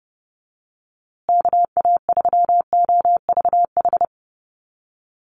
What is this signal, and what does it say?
Morse code, heard as a keyed tone, about 30 wpm, 700 Hz. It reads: KA3OVH
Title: USA Call Signs Repeated 2x 30wpm